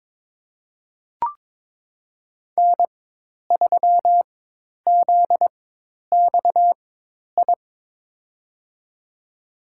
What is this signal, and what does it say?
Morse code, heard as a keyed tone, about 22 wpm, spaced as for 10 wpm, 700 Hz. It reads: N3ZXI
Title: USA Call Signs - 5 Characters - 4x Character Spacing - 22wpm